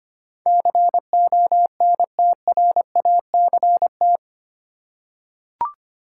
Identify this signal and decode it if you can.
Morse code, heard as a keyed tone, about 25 wpm, 700 Hz. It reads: CONTRACT
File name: Eight Letter Words - 25wpm